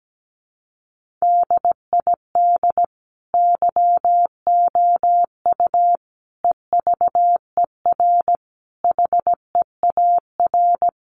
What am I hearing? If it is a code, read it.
Morse code, heard as a keyed tone, about 17 wpm, 700 Hz. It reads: DID YOU EVER HEAR
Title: Sentences from Top 1400 Words 17wpm